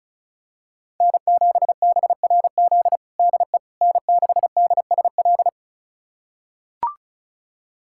Morse code, heard as a keyed tone, about 34 wpm, 700 Hz. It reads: N7BRZ DE N6DSL
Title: USA Callsign DE Callsign - 5 Characters - 35wpm